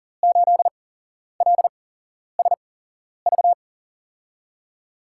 Morse code, heard as a keyed tone, about 40 wpm, spaced as for 11 wpm, 700 Hz. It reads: 8LSV
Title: USA Call Signs - 6 Characters - 8x Character Spacing - 40wpm